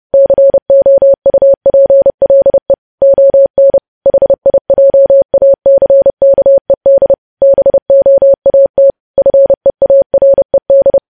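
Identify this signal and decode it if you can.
Morse code, heard as a keyed tone, about 30 wpm, 550 Hz. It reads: COUPLE ON HIJACKED BOAT FEARED
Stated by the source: News headlines sent in Morse code at 30.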